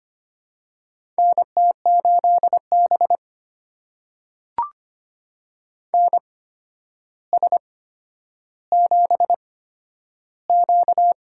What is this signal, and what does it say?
Morse code, heard as a keyed tone, about 25 wpm, spaced as for 11 wpm, 700 Hz. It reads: NT8B NS7Q
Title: USA Call Signs - 4 Characters - 8x Character Spacing - 25wpm